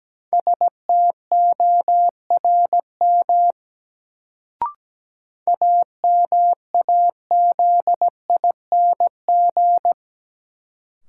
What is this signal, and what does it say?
Morse code, heard as a keyed tone, about 17 wpm, 700 Hz. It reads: STORM AMAZING
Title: Sentences from Top 1500 Words - New Word Drill - 17wpm